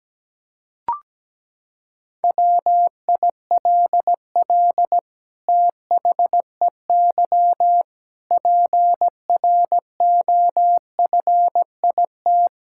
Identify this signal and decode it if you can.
Morse code, heard as a keyed tone, about 17 wpm, 700 Hz. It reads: WILL THEY PROFIT
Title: Sentences from Top 1000 Words - Review Part 6 - 17wpm